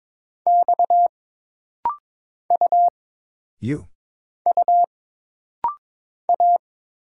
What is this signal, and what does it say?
Morse code, heard as a keyed tone, about 22 wpm, 700 Hz. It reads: X U U A
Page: This rapid-fire timing is geared to push your ICR (Instant Character Recognition) skills.